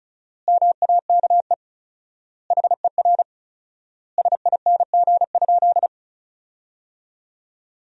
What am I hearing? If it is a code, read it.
Morse code, heard as a keyed tone, about 36 wpm, 700 Hz. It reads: MAKE HER SING?
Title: 4x Word-Spacing - Sentences from Top 400 Words - 35wpm